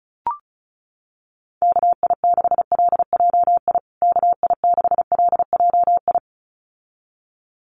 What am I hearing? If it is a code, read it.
Morse code, heard as a keyed tone, about 34 wpm, 700 Hz. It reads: KI6LJI KI6LJI
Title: USA Call Signs Repeated 2x 35wpm